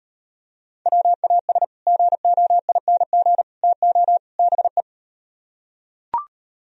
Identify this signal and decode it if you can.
Morse code, heard as a keyed tone, about 38 wpm, 700 Hz. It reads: WAS GOING TO BE